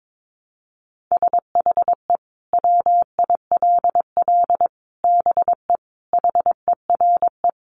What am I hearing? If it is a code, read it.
Morse code, heard as a keyed tone, about 22 wpm, 700 Hz. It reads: SHE WILL BE HERE
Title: Sentences from Top 600 Words 22wpm